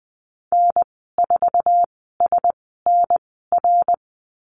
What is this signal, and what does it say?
Morse code, heard as a keyed tone, about 20 wpm, spaced as for 15 wpm, 700 Hz. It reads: N4SNR
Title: 055 Lesson 10g - Letter r - Call Signs 20wpm